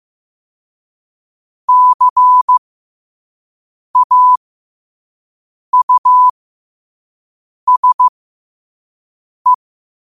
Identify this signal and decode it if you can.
Morse code, heard as a keyed tone, about 15 wpm, spaced as for 5 wpm, 1000 Hz. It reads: CAUSE
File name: qotdfiles / 5wpm